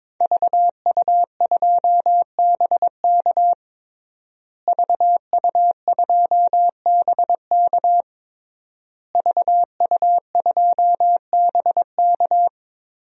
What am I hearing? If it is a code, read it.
Morse code, heard as a keyed tone, about 22 wpm, 700 Hz. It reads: VU2BK VU2BK VU2BK
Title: DXCC-100 Call Signs - 5 Characters - Repeated 3x - 22wpm